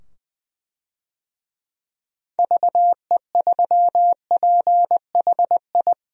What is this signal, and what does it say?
Morse code, heard as a keyed tone, about 20 wpm, 700 Hz. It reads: VE3PHI
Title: DXCC-100 Call Signs - 6 Characters - 20wpm